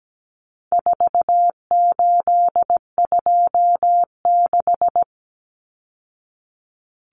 Morse code, heard as a keyed tone, about 17 wpm, 700 Hz. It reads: 4826
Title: Numbers - 4-Digit 17wpm